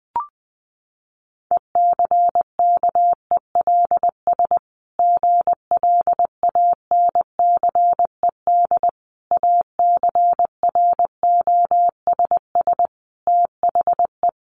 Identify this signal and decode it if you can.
Morse code, heard as a keyed tone, about 20 wpm, 700 Hz. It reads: ECKELS GLANCED ACROSS THE
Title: A Sound Of Thunder 20wpm